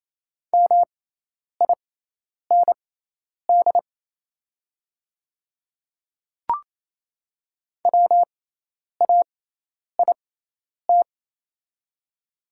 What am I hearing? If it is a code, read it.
Morse code, heard as a keyed tone, about 27 wpm, spaced as for 10 wpm, 690 Hz. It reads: MIND WAIT
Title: 6x Character Spacing - Top 500 Words - 28wpm